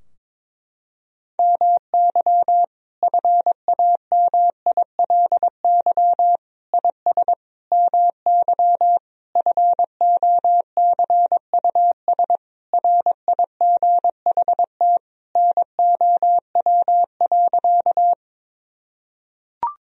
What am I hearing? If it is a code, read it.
Morse code, heard as a keyed tone, about 22 wpm, 700 Hz. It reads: MY FAMILY IS MY FOCUS RIGHT NOW.
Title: Sentences from Top 1000 Words - Review Part 4 - 22wpm